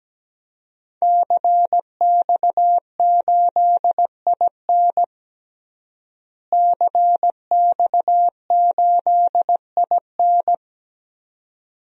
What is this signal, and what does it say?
Morse code, heard as a keyed tone, about 17 wpm, 700 Hz. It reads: CX8IN CX8IN
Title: DXCC-100 Call Signs - 5 Characters - Repeated 2x - 17wpm